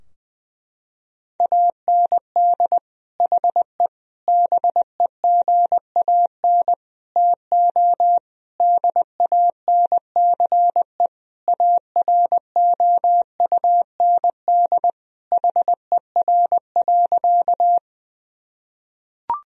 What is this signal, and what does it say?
Morse code, heard as a keyed tone, about 21 wpm, 700 Hz. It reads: AND HE BEGAN TO DANCE AROUND HER.
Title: Sentences from Top 1000 Words - Review Part 4 - 20wpm